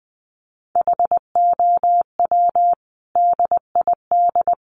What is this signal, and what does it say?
Morse code, heard as a keyed tone, about 20 wpm, 700 Hz. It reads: HOW DID
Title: Sentences from Top 100 Words 20wpm